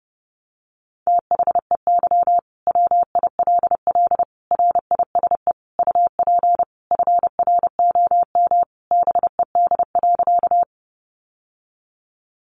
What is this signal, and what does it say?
Morse code, heard as a keyed tone, about 30 wpm, 700 Hz. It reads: THEY WILL RISE UP FROM BED.
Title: Sentences from Top 900 Words 30wpm